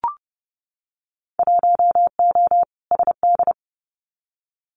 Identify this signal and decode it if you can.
Morse code, heard as a keyed tone, about 30 wpm, 700 Hz. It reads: 1O SD
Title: Winter Field Day Exchanges Encore 30wpm